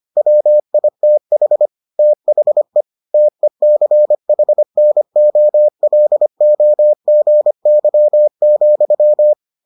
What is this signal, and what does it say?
Morse code, heard as a keyed tone, about 25 wpm, 600 Hz. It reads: WITH THE TECHNOLOGY,